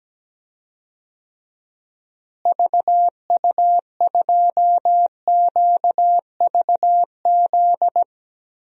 Morse code, heard as a keyed tone, about 17 wpm, 700 Hz. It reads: VU2QVZ
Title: DXCC-100 Call Signs - 6 Characters - Repeated 3x - 17wpm